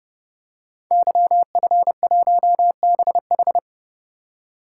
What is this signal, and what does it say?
Morse code, heard as a keyed tone, about 30 wpm, 700 Hz. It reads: YF1BH
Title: Code Groups - 5 Letter:Numbers - 30wpm